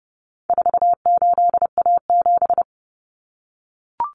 Morse code, heard as a keyed tone, about 30 wpm, 700 Hz. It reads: 48A7
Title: Common Cut Numbers - 4-Digit 30wpm